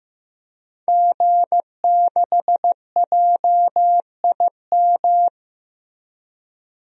Morse code, heard as a keyed tone, about 15 wpm, 700 Hz. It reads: G6JIM
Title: DXCC-100 Call Signs - 5 Characters - Repeated 2x - 15wpm